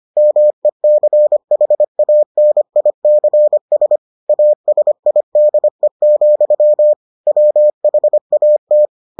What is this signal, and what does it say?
Morse code, heard as a keyed tone, about 25 wpm, 600 Hz. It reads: MECHANICS ASIDE, WHAT